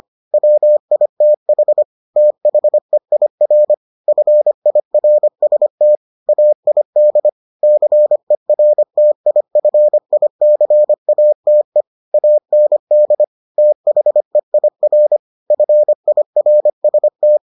Morse code, heard as a keyed tone, about 25 wpm, 600 Hz. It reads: WITH THEIR FIRST AID CERTIFICATE AND THEIR FIRST